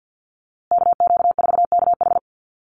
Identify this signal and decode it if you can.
Morse code, heard as a keyed tone, about 48 wpm, 700 Hz. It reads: KQ4KH
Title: USA Call Signs Repeated 2x 50wpm